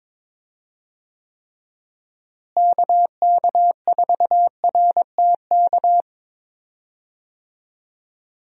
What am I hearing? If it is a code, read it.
Morse code, heard as a keyed tone, about 21 wpm, 700 Hz. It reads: KK4RTK
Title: USA Call Signs - 6 Characters - Repeated 2x - 22wpm